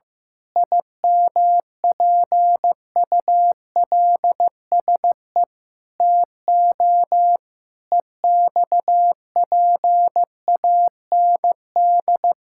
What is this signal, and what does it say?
Morse code, heard as a keyed tone, about 15 wpm, 700 Hz. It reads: IMPULSE TO EXPAND
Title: Pangram Challenge v2 - 15wpm